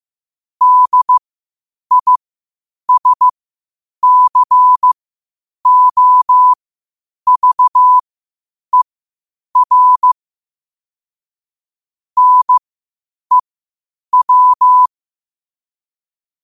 Morse code, heard as a keyed tone, about 15 wpm, spaced as for 8 wpm, 1000 Hz. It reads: DISCOVER NEW
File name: qotdfiles / 8wpm